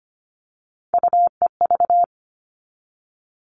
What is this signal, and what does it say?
Morse code, heard as a keyed tone, about 25 wpm, 700 Hz. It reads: UEV
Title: All Cut Numbers - 3-Digit 25wpm